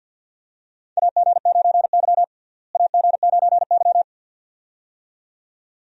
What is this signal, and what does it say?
Morse code, heard as a keyed tone, about 46 wpm, 700 Hz. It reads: AG9Y AG9Y
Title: USA Call Signs - 4 Characters - Repeated 2x - 50wpm